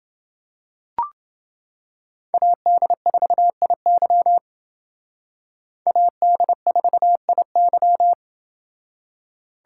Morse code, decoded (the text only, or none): AD4IY AD4IY